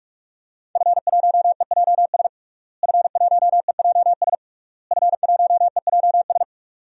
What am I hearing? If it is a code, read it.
Morse code, heard as a keyed tone, about 42 wpm, 700 Hz. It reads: F1EJS F1EJS F1EJS
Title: DXCC-100 Call Signs - 5 Characters - Repeated 3x - 45wpm